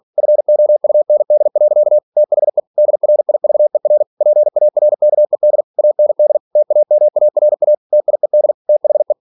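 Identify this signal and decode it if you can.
Morse code, heard as a keyed tone, about 46 wpm, 600 Hz. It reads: FOUND. THE DRIVER PARKED AND TAMARA TIED THE
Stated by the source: Practice text from an English novel sent at 47 WPM.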